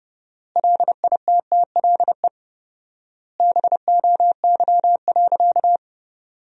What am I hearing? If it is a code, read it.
Morse code, heard as a keyed tone, about 31 wpm, 700 Hz. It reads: LITTLE BOY.
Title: 4x Word-Spacing - Sentences from Top 200 Words - 30wpm